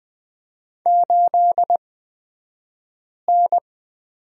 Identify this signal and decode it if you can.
Morse code, heard as a keyed tone, about 20 wpm, spaced as for 5 wpm, 700 Hz. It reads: 8N
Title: 249 Lesson 40d - BK - Two Characters 20wpm